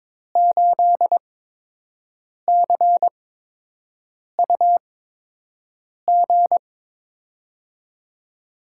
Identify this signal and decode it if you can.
Morse code, heard as a keyed tone, about 22 wpm, spaced as for 6 wpm, 700 Hz. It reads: 8CUG
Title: USA Call Signs - 5 Characters - 8x Character Spacing - 22wpm